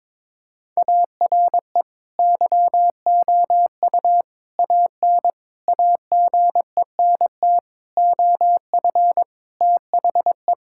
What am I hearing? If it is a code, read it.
Morse code, heard as a keyed tone, about 21 wpm, 700 Hz. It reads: ARE YOU AN AGENT OF THE
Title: Sentences from Top 1600 Words 22wpm